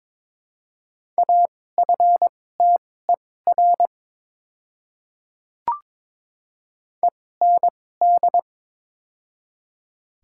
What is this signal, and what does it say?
Morse code, heard as a keyed tone, about 21 wpm, spaced as for 16 wpm, 700 Hz. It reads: AFTER END
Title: 2x Character Spacing - Top 300 Words - 22wpm